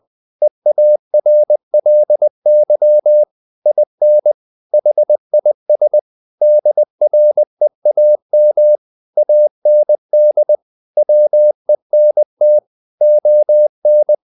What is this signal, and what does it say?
Morse code, heard as a keyed tone, about 20 wpm, 600 Hz. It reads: EARLY IN HIS DREAM AND WENT ON
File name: the old man and the sea / the old man and the sea part 1 20wpm